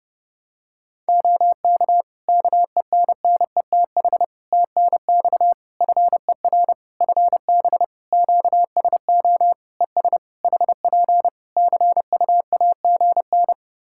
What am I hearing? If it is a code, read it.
Morse code, heard as a keyed tone, about 30 wpm, 700 Hz. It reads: OK KENNETH TNX FER FB QSO ES HP CUAGN
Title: Practice QSO - Exchange 4 - Ending - 30wpm